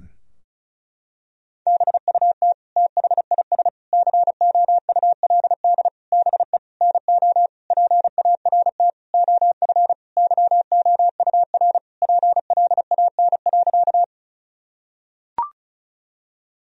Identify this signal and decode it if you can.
Morse code, heard as a keyed tone, about 36 wpm, 700 Hz. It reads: BUT THIS COULD BE NO PART OF YOUR PLAN.
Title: Sentences from Top 1000 Words - Review Part 2 - 35wpm